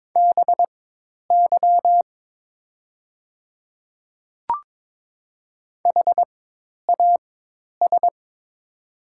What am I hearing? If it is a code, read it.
Morse code, heard as a keyed tone, about 22 wpm, spaced as for 10 wpm, 700 Hz. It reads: BY HAS
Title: 4x Character Spacing - Top 100 Words - 22wpm